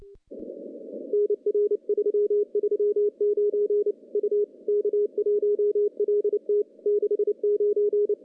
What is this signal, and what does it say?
Morse code, heard as a keyed tone, about 29 wpm, 410 Hz. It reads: NR 339 U K1LT 69
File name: CW-SS-k5tr